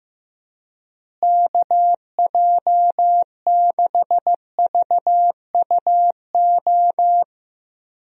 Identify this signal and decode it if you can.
Morse code, heard as a keyed tone, about 15 wpm, 700 Hz. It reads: KJ6VUO
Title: USA Call Signs - 6 Characters - Repeated 3x - 15wpm